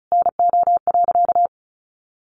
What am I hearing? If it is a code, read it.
Morse code, heard as a keyed tone, about 34 wpm, 700 Hz. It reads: NO.
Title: A Sound Of Thunder 35wpm